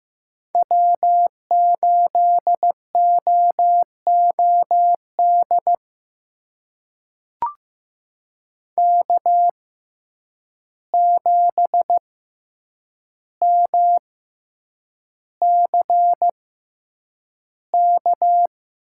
Morse code, heard as a keyed tone, about 15 wpm, spaced as for 8 wpm, 700 Hz. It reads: W8OOD K7MCK
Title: USA Call Signs - 5 Characters - 6x Character Spacing - 15wpm